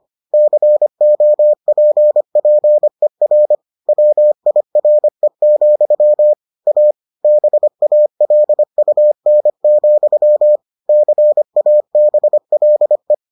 Morse code, heard as a keyed tone, about 25 wpm, 600 Hz. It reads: COPPER WIRE, A BALUN, CABLE